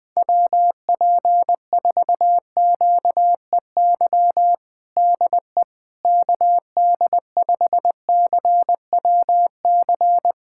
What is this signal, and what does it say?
Morse code, heard as a keyed tone, about 21 wpm, 700 Hz. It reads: WP4QEY DE KD5CWC
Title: USA Callsign DE Callsign 20wpm